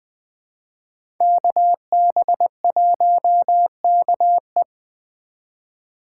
Morse code, heard as a keyed tone, about 20 wpm, 700 Hz. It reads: KB1KE
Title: USA Call Signs - 5 Characters - 2x Character Spacing - 20wpm